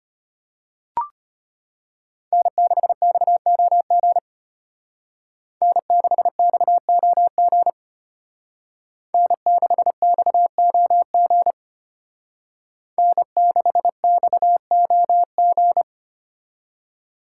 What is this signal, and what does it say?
Morse code, heard as a keyed tone, about 33 wpm, 700 Hz. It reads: N6XOG N6XOG N6XOG N6XOG